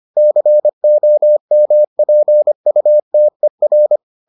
Morse code, heard as a keyed tone, about 25 wpm, 600 Hz. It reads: COMPUTER